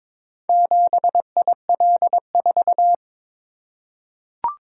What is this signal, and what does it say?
Morse code, heard as a keyed tone, about 22 wpm, 700 Hz. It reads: 7IL4
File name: practice / 22wpm / Four Letter-Number Combinations - 22wpm